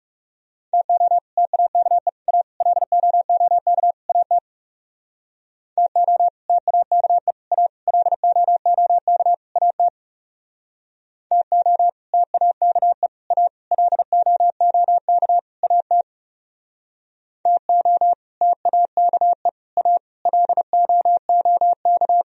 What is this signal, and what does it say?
Morse code, heard as a keyed tone, about 36 wpm, 700 Hz. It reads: TO TAKE A LOOK AT TO TAKE A LOOK AT TO TAKE A LOOK AT TO TAKE A LOOK
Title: Speed-Racing - Sets of 5 Words 30wpm